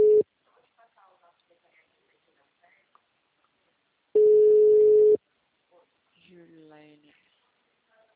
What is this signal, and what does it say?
Ringback tone (Europe, 425 Hz), with speech.